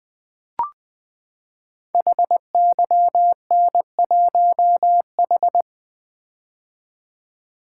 Morse code, heard as a keyed tone, about 20 wpm, 700 Hz. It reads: HYN1H
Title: Code Groups - 5 Letter:Numbers - 20wpm